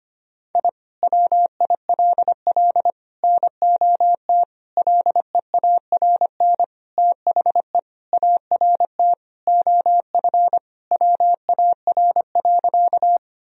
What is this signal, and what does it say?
Morse code, heard as a keyed tone, about 25 wpm, 700 Hz. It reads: I WILL NOT LEARN THE ART OF WAR.
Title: Sentences from Top 1000 Words - Review Part 3 - 25wpm